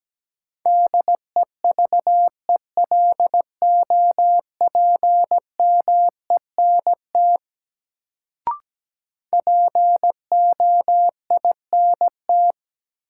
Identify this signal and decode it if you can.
Morse code, heard as a keyed tone, about 17 wpm, 700 Hz. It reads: DEVELOPMENT POINT
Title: Sets of 3 Words - New Word Drill - 17wpm